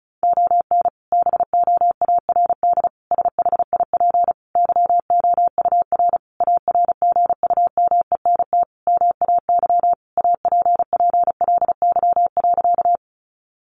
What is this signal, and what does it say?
Morse code, heard as a keyed tone, about 34 wpm, 700 Hz. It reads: ON BOARD SHIP YOUR ARGUMENT MAY APPLY.
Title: Sentences from Top 1400 Words 35wpm